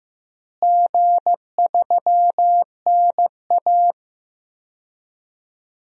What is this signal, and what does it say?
Morse code, heard as a keyed tone, about 15 wpm, 700 Hz. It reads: G3NA
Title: DXCC-100 Call Signs - 4 Characters - Repeated 2x - 15wpm